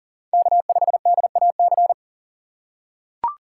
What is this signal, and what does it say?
Morse code, heard as a keyed tone, about 37 wpm, 700 Hz. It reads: K5DAC
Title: USA Call Signs - 5 Characters - 3x Character Spacing - 40wpm